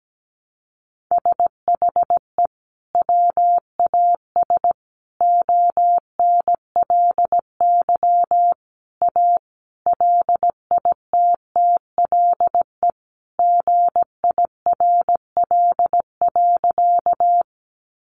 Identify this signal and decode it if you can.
Morse code, heard as a keyed tone, about 17 wpm, 700 Hz. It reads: SHE WAS ONLY A LITTLE GIRL.
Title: Sentences from Top 400 Words 17wpm